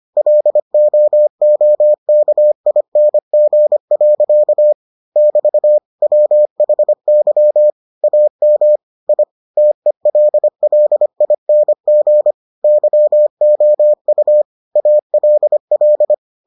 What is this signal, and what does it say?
Morse code, heard as a keyed tone, about 25 wpm, 600 Hz. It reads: LOOKING. <BT> WHY AM I TELLING YOU ALL